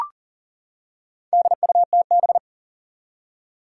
Morse code, heard as a keyed tone, about 42 wpm, 690 Hz. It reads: DUTB
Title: Four Challenging Letter-Number Combinations 40wpm